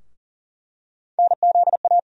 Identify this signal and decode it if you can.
Morse code, heard as a keyed tone, about 40 wpm, 700 Hz. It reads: NZA
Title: USA Call Sign Postfixes - 3 Letters 40wpm